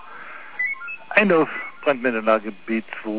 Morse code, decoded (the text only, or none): none